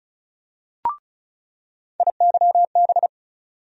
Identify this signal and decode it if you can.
Morse code, heard as a keyed tone, about 36 wpm, 700 Hz. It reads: IYB
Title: USA Call Sign Postfixes - 3 Letters 35wpm